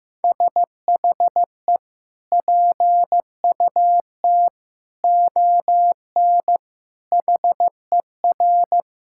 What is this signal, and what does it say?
Morse code, heard as a keyed tone, about 15 wpm, 700 Hz. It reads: SHE PUT ON HER
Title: Sentences from Top 1600 Words 15wpm